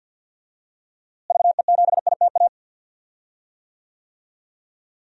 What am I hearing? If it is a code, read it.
Morse code, heard as a keyed tone, about 52 wpm, 700 Hz. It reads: VE7ITA
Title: DXCC-100 Call Signs - 6 Characters - Repeated 2x - 50wpm